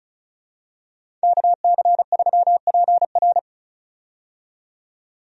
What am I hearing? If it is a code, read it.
Morse code, heard as a keyed tone, about 36 wpm, 700 Hz. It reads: KC3PR
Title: USA Call Signs - 5 Characters - 35wpm